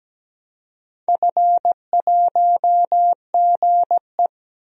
Morse code, heard as a keyed tone, about 17 wpm, 700 Hz. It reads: F1GE
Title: DXCC-100 Call Signs - 4 Characters - Repeated 2x - 17wpm